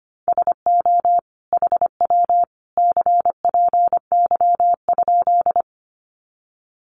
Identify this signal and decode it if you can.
Morse code, heard as a keyed tone, about 25 wpm, 700 Hz. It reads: SO HW CPY?
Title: Common QSO Phrases 25wpm